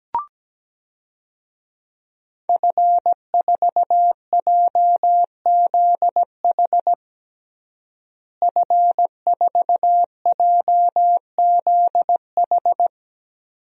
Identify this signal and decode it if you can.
Morse code, heard as a keyed tone, about 17 wpm, 700 Hz. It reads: F4JZH F4JZH
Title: DXCC-100 Call Signs - 5 Characters - Repeated 2x - 17wpm